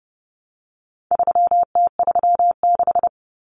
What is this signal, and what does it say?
Morse code, heard as a keyed tone, about 30 wpm, 700 Hz. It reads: 3T36
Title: Common Cut Numbers - 4-Digit 30wpm